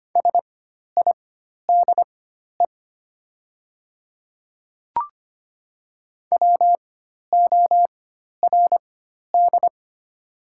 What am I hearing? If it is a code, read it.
Morse code, heard as a keyed tone, about 25 wpm, spaced as for 12 wpm, 690 Hz. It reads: SIDE WORD
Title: 4x Character Spacing - Top 100 Words - 25wpm